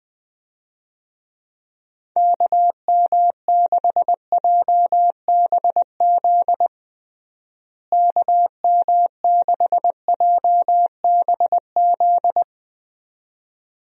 Morse code, heard as a keyed tone, about 20 wpm, 700 Hz. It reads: KM6JBZ KM6JBZ
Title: USA Call Signs - 6 Characters - Repeated 2x - 20wpm